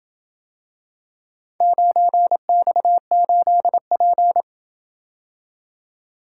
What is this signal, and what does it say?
Morse code, heard as a keyed tone, about 27 wpm, 700 Hz. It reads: 9X8P